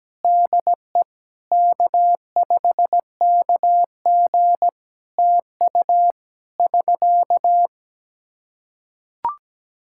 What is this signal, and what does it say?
Morse code, heard as a keyed tone, about 17 wpm, 700 Hz. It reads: DE K5KG TU <SK>
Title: Practice QSO - Exchange 4 - Ending - 17wpm